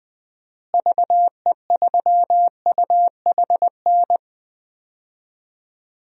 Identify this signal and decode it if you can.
Morse code, heard as a keyed tone, about 21 wpm, 700 Hz. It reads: VE3UHN
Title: DXCC-100 Call Signs - 6 Characters - Repeated 2x - 20wpm